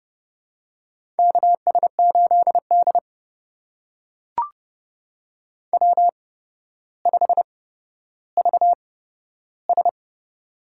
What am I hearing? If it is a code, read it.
Morse code, heard as a keyed tone, about 31 wpm, spaced as for 13 wpm, 700 Hz. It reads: KS8D W5VS
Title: USA Call Signs - 4 Characters - 8x Character Spacing - 30wpm